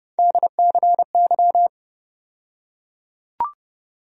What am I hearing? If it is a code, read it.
Morse code, heard as a keyed tone, about 30 wpm, 700 Hz. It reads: DCY
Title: USA Call Sign Postfixes - 3 Letters 30wpm